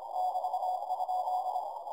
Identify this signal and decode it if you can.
Morse code, heard as a keyed tone, about 26 wpm, 740 Hz, with noise. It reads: <BT>?
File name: 30mtr dir noisereduction EA6BH CW